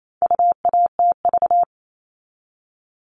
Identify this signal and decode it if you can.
Morse code, heard as a keyed tone, about 28 wpm, 700 Hz. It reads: UATV
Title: All Cut Numbers - 4-Digit 28wpm